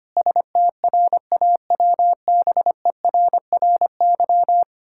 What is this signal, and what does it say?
Morse code, heard as a keyed tone, about 25 wpm, 700 Hz. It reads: STRAWBERRY
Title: Pangram Challenge v2 - 25wpm